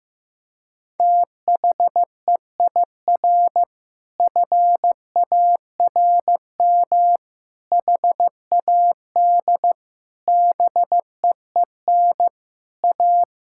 Morse code, heard as a keyed tone, about 16 wpm, 700 Hz. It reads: THEIR FARM HAD BEEN A
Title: Sentences from Top 1000 Words - Review Part 6 - 15wpm